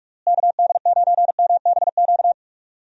Morse code, heard as a keyed tone, about 46 wpm, 700 Hz. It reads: KD9MBQ
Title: USA Call Signs - 6 Characters - Repeated 2x - 45wpm